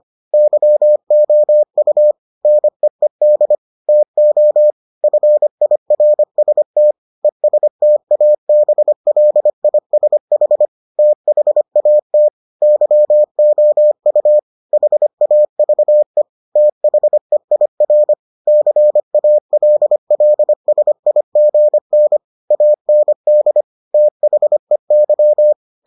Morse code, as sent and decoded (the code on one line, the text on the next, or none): -.-- --- ..- / -. . -.. / - --- / ..-. .. .-. ... - / . ... - .- -... .-.. .. ... .... / - .... .- - / -.-- --- ..- / .... .- ...- . / - .... .. .-. / -.-. .- .-.. .-.. ... .. --. -. / .- -. -.. / - .... -.--
YOU NEED TO FIRST ESTABLISH THAT YOU HAVE THEIR CALLSIGN AND THEY